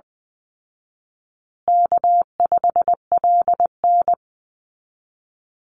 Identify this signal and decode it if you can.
Morse code, heard as a keyed tone, about 20 wpm, 700 Hz. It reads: K5LN
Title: Easy Call Signs Repeated 3x - 20wpm